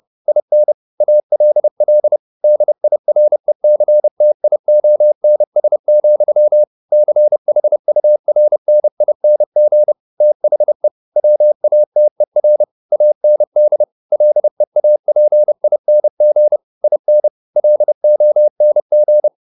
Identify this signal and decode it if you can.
Morse code, heard as a keyed tone, about 30 wpm, 600 Hz. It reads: IN ALL DIRECTIONS, CHURNING THE WATER AND LEAPING IN LONG
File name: the old man and the sea / the old man and the sea 1 30wpm